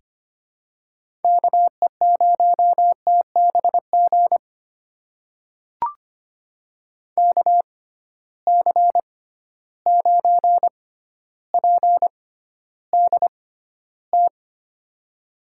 Morse code, heard as a keyed tone, about 25 wpm, spaced as for 13 wpm, 700 Hz. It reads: KE0TBG KC9PDT
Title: USA Call Signs - 6 Characters - 6x Character Spacing - 25wpm